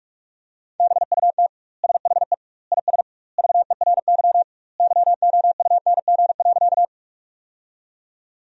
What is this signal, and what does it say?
Morse code, heard as a keyed tone, about 46 wpm, 700 Hz. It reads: BUT SHE IS VERY YOUNG.
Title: 2x Word-Spacing - Sentences from Top 400 Words - 45wpm